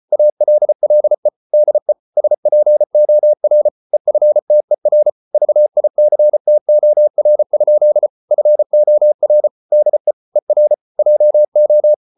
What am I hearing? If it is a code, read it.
Morse code, heard as a keyed tone, about 34 wpm, 600 Hz. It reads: ALLE DE SPOR EFTER VICTOR? FOR DE ER JO
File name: Ferie QRQ 34wpm